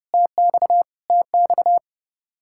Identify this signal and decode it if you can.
Morse code, heard as a keyed tone, about 30 wpm, 700 Hz. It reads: TX TX
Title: Practice QSO - Exchange 1 - Beginning - 30wpm